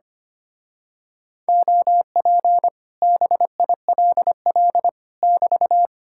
Morse code, heard as a keyed tone, about 24 wpm, 700 Hz. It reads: OP BILL <BT>
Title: Speed-Racing Common QSO Phrases - 25wpm